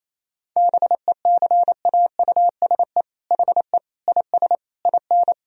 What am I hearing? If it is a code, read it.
Morse code, heard as a keyed tone, about 28 wpm, 700 Hz. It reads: BECAUSE HE IS IN
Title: Sentences from Top 1000 Words - Review Part 3 - 28wpm